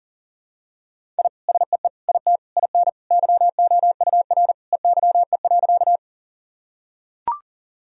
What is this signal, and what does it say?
Morse code, heard as a keyed tone, about 40 wpm, 700 Hz. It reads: I SEE IT IN YOUR EYE.
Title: Sentences from Top 1000 Words - Review Part 1 - 40wpm